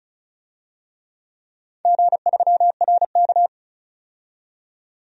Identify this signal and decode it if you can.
Morse code, heard as a keyed tone, about 36 wpm, 700 Hz. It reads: G3RK
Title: DXCC-100 Call Signs - 4 Characters - 35wpm